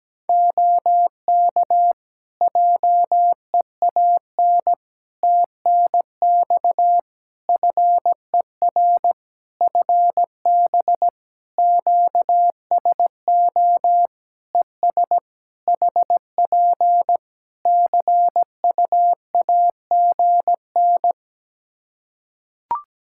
Morse code, heard as a keyed tone, about 17 wpm, 700 Hz. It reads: OK JEAN TNX FER FB QSO ES HP CUAGN
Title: Practice QSO - Exchange 4 - Ending - 17wpm